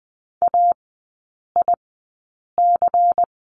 Morse code, heard as a keyed tone, about 20 wpm, spaced as for 8 wpm, 700 Hz. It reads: AIC
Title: 093 Lesson 16e - Letter c - Three Characters 20wpm